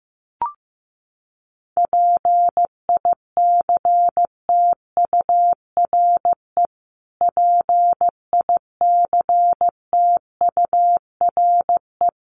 Morse code, heard as a keyed tone, about 15 wpm, 700 Hz. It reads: PICTURE PICTURE
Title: Top 300 Words Repeated 15wpm